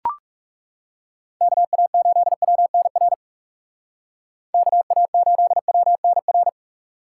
Morse code, heard as a keyed tone, about 38 wpm, 700 Hz. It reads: KA8WNR KA8WNR